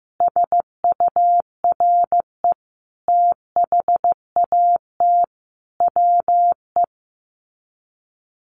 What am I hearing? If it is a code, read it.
Morse code, heard as a keyed tone, about 15 wpm, 700 Hz. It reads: SURE THAT WE
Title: Sets of 5 Words Encore 15wpm